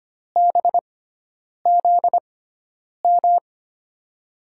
Morse code, heard as a keyed tone, about 25 wpm, spaced as for 9 wpm, 700 Hz. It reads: BZM
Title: USA Call Signs - 6 Characters - 6x Character Spacing - 25wpm